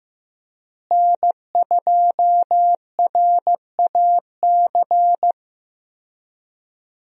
Morse code, heard as a keyed tone, about 15 wpm, 700 Hz. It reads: N2RAC
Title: USA Call Signs - 5 Characters - Repeated 3x - 15wpm